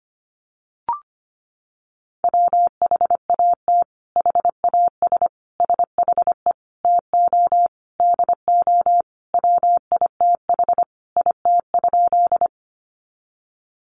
Morse code, heard as a keyed tone, about 25 wpm, 700 Hz. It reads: WHAT HAS SHE TO DO WITH IT?
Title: Sentences from Top 100 Words 25wpm